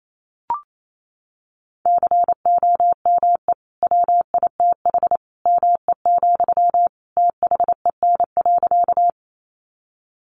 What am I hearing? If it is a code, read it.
Morse code, heard as a keyed tone, about 28 wpm, 700 Hz. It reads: COME WITH ME, THEN.
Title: Sentences from Top 200 Words 28wpm